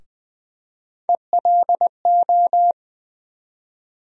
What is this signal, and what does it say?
Morse code, heard as a keyed tone, about 20 wpm, 700 Hz. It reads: ELO
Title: USA Call Sign Postfixes - 3 Letters 20wpm